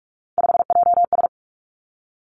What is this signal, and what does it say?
Morse code, heard as a keyed tone, about 46 wpm, 700 Hz. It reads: 5JS
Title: Three Challenging Letter-Number Combinations 45wpm